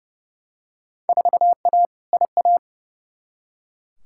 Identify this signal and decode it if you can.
Morse code, heard as a keyed tone, about 32 wpm, 700 Hz. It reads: 4A IA
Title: Field Day Exchanges - 30wpm